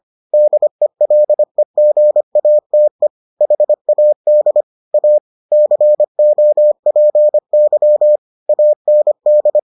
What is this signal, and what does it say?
Morse code, heard as a keyed tone, about 25 wpm, 600 Hz. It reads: DELEGATE HAD A COPY AND